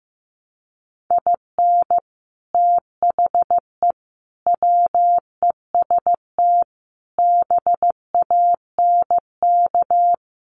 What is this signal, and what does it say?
Morse code, heard as a keyed tone, about 15 wpm, 700 Hz. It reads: IN THE WEST BANK
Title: Sets of 5 Words Encore2 15wpm